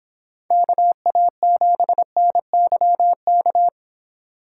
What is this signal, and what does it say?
Morse code, heard as a keyed tone, about 26 wpm, 700 Hz. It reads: KA7NYK